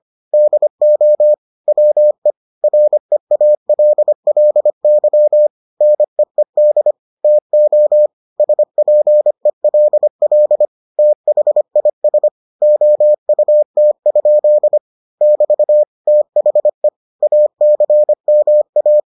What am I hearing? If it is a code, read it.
Morse code, heard as a keyed tone, about 25 wpm, 600 Hz. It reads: DO WE REALLY NEED TO SPELL THIS OUT? <BT> THE ACMA